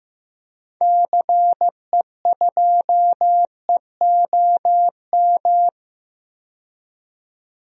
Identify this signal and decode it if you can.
Morse code, heard as a keyed tone, about 15 wpm, 700 Hz. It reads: CE2EOM
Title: DXCC-100 Call Signs - 6 Characters - 15wpm